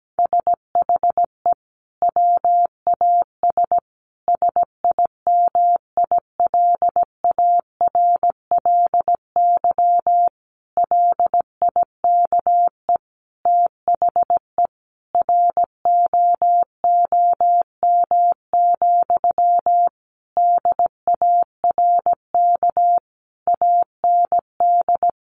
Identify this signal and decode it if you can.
Morse code, heard as a keyed tone, about 17 wpm, 700 Hz. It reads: SHE WAS SIMILARLY LIKE THE ROOM, DARK AND
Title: Sentences from Top 1400 Words 17wpm